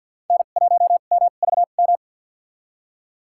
Morse code, heard as a keyed tone, about 47 wpm, 700 Hz. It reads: N 1 M V M
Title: USA Call Signs - 5 Characters - 2x Character Spacing - 50wpm